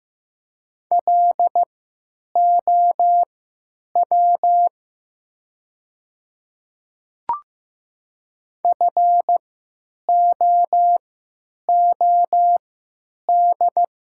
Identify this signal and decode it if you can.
Morse code, heard as a keyed tone, about 15 wpm, spaced as for 9 wpm, 700 Hz. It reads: LOW FOOD
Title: 3x Character Spacing - Top 300 Words - 15wpm